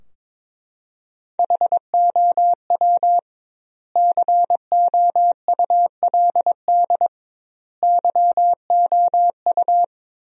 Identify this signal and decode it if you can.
Morse code, heard as a keyed tone, about 22 wpm, 700 Hz. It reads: HOW COULD YOU
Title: 2x Word-Spacing - Sentences from Top 300 Words - 22wpm